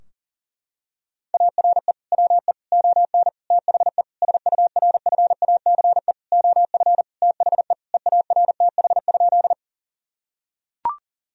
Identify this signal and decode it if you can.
Morse code, heard as a keyed tone, about 40 wpm, 700 Hz. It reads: ARE WE ON THE SURFACE OF THE EARTH?
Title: Sentences from Top 1000 Words - Review Part 3 - 40wpm